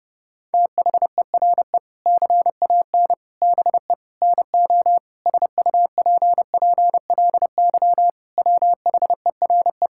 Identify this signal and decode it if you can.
Morse code, heard as a keyed tone, about 30 wpm, 700 Hz. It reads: THERE CAN BE NO SUPPLY WHERE
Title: Sentences from Top 1000 Words - Review Part 5 - 30wpm